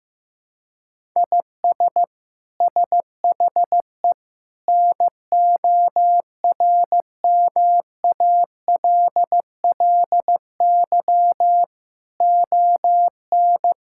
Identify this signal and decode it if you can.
Morse code, heard as a keyed tone, about 15 wpm, 700 Hz. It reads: IS SHE NORMALLY ON
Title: Sentences from Top 1000 Words - Review Part 6 - 15wpm